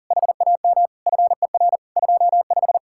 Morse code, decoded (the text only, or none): HAM FER 25